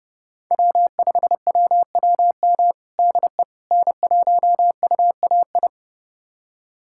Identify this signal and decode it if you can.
Morse code, heard as a keyed tone, about 28 wpm, 700 Hz. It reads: W5WWM DE N1UAI
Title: USA Callsign DE Callsign - 5 Characters - 30wpm